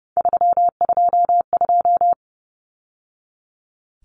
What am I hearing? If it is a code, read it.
Morse code, heard as a keyed tone, about 30 wpm, 700 Hz. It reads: 322
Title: Numbers - 3-Digit 30wpm